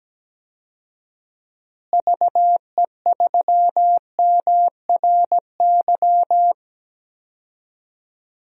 Morse code, heard as a keyed tone, about 17 wpm, 700 Hz. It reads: VE3MRY
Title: DXCC-100 Call Signs - 6 Characters - Repeated 3x - 17wpm